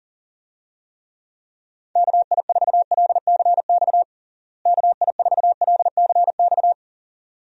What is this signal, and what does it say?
Morse code, heard as a keyed tone, about 40 wpm, 700 Hz. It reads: KI4LCX KI4LCX
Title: USA Call Signs - 6 Characters - Repeated 2x - 40wpm